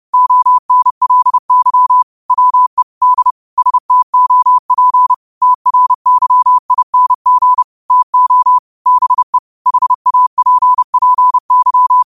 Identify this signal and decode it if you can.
Morse code, heard as a keyed tone, about 30 wpm, 1000 Hz. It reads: ONLY WE D STOP TRYING TO BE HAPPY
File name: qotdfiles / 30wpm